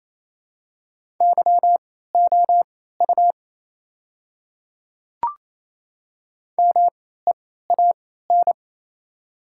Morse code, heard as a keyed tone, about 27 wpm, spaced as for 16 wpm, 700 Hz. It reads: YOU MEAN
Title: 3x Character Spacing - Top 200 Words - 28wpm